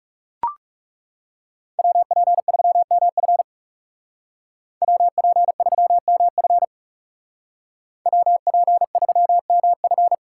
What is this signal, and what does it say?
Morse code, heard as a keyed tone, about 37 wpm, 700 Hz. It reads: WP3MF WP3MF WP3MF
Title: Speed Racing - USA 2x2 Call Signs 30wpm